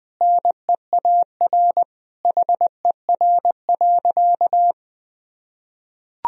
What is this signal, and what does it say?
Morse code, heard as a keyed tone, about 20 wpm, 700 Hz. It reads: NEAR HER.
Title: Sentences from Top 1000 Words - Review Part 4 - 20wpm